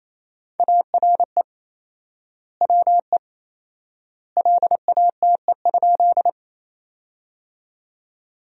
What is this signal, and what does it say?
Morse code, heard as a keyed tone, about 28 wpm, 700 Hz. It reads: ARE WE LATE?
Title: 4x Word-Spacing - Sentences from Top 300 Words - 28wpm